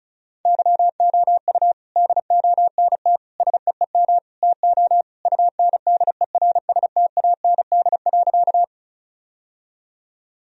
Morse code, heard as a keyed tone, about 36 wpm, 700 Hz. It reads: YOU DONT SEEM TO UNDERSTAND.
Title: Sentences from Top 1000 Words - Review Part 1 - 35wpm